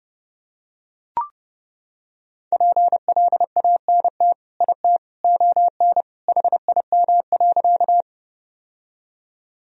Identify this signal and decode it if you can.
Morse code, heard as a keyed tone, about 32 wpm, 700 Hz. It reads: PLANT IT ON HIM.
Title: Sentences from Top 1000 Words - Review Part 1 - 30wpm